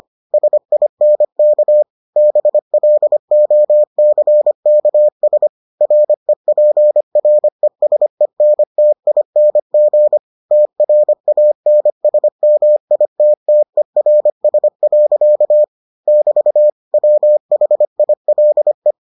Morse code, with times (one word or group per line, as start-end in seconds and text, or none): 0.3-1.8 SINK
2.2-5.5 BLOCKS
5.8-10.2 REPRESENTING
10.5-15.6 TRANSMITTERS.
16.1-16.7 <BT>
16.9-18.9 WHILE